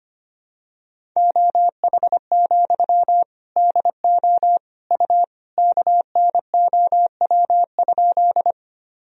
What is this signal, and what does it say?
Morse code, heard as a keyed tone, about 25 wpm, 700 Hz. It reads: OH, DO U KNOW?
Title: Sentences from Top 500 Words and Abbreviations 25wpm